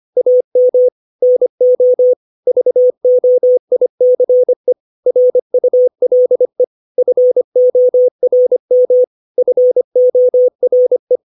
Morse code, heard as a keyed tone, about 25 wpm, 500 Hz. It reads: AM NO VOICE RULE FORM FORE
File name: 632 words / 632 Words 25WPM